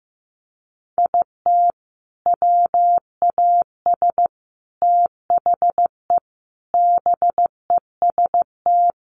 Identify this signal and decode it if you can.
Morse code, heard as a keyed tone, about 15 wpm, 700 Hz. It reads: IT WAS THE BEST
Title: Sentences from Top 1300 Words 15wpm